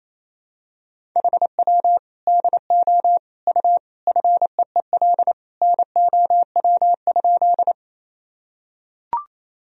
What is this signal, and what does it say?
Morse code, heard as a keyed tone, about 28 wpm, 700 Hz. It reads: HW DO U FEEL NOW?
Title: Sentences from Top 400 Words and Abbreviations 28wpm